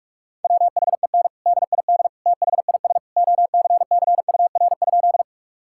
Morse code, heard as a keyed tone, about 46 wpm, 700 Hz. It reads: WHEN DID THIS OCCUR?
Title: Sentences from Top 1000 Words - Review Part 6 - 45wpm